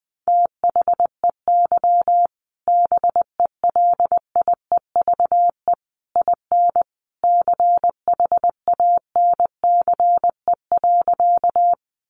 Morse code, heard as a keyed tone, about 20 wpm, 700 Hz. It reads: THEY BELIEVE IN CHANCE.
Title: Sentences from Top 1000 Words 20wpm